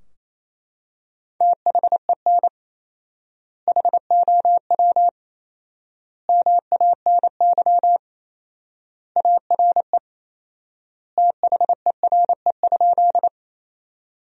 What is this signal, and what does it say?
Morse code, heard as a keyed tone, about 27 wpm, 700 Hz. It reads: THEN HOW MANY ARE THERE?
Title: 4x Word-Spacing - Sentences from Top 100 Words - 28wpm